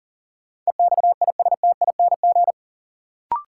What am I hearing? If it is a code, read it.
Morse code, heard as a keyed tone, about 40 wpm, 700 Hz. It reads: EXISTING
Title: Sentences from Top 1500 Words - New Word Drill Alt - 40wpm